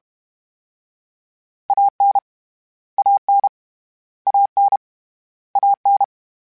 Morse code, heard as a keyed tone, about 32 wpm, 800 Hz. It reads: AN AN AN AN